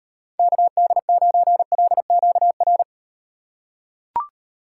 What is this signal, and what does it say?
Morse code, heard as a keyed tone, about 38 wpm, 700 Hz. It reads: KD9LQR